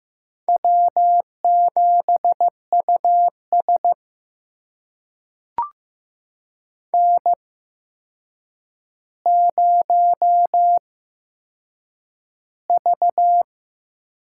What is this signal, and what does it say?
Morse code, heard as a keyed tone, about 15 wpm, spaced as for 7 wpm, 700 Hz. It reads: W7US N0V
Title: USA Call Signs - 4 Characters - 8x Character Spacing - 15wpm